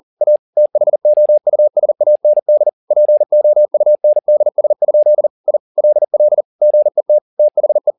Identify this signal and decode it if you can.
Morse code, heard as a keyed tone, about 40 wpm, 600 Hz. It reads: A THOUSAND POUNDS? I LL GET THE